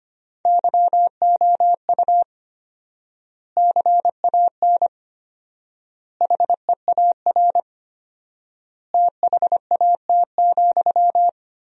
Morse code, heard as a keyed tone, about 25 wpm, 700 Hz. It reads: YOU CAN HEAR THAT,
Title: 4x Word-Spacing - Sentences from Top 500 Words - 25wpm